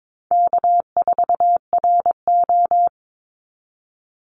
Morse code, heard as a keyed tone, about 22 wpm, 700 Hz. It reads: K4RO
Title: Easy Call Signs Repeated 2x - 22wpm